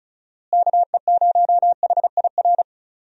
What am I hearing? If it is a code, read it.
Morse code, heard as a keyed tone, about 36 wpm, 700 Hz. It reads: KE0HIR
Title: USA Call Signs - 6 Characters - Repeated 3x - 35wpm